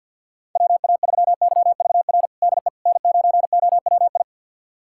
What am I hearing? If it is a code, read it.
Morse code, heard as a keyed tone, about 52 wpm, 700 Hz. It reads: WA3YVR DE N9OWI
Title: USA Callsign DE Callsign 50wpm